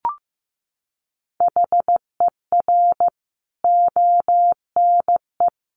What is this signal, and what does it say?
Morse code, heard as a keyed tone, about 15 wpm, 700 Hz. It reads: HER ONE
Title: Sentences from Top 1500 Words 15wpm